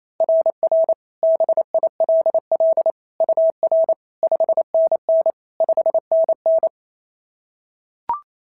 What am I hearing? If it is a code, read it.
Morse code, heard as a keyed tone, about 28 wpm, 690 Hz. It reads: RR BILL UR 5NN 5NN
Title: SOTA QSO Practice 28wpm